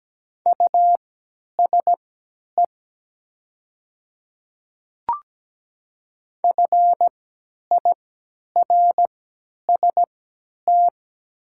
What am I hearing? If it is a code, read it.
Morse code, heard as a keyed tone, about 17 wpm, spaced as for 10 wpm, 700 Hz. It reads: USE FIRST
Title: 3x Character Spacing - Top 100 Words - 17wpm